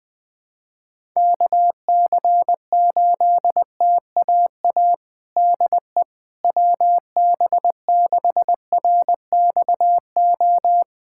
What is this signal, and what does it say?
Morse code, heard as a keyed tone, about 21 wpm, 700 Hz. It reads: KC8TAA DE WB6RXO
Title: USA Callsign DE Callsign - 6 Characters - 20wpm